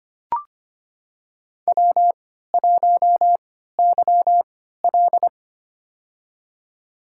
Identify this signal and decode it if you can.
Morse code, heard as a keyed tone, about 26 wpm, spaced as for 14 wpm, 700 Hz. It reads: W1YL
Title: USA Call Signs - 4 Characters - 3x Character Spacing - 25wpm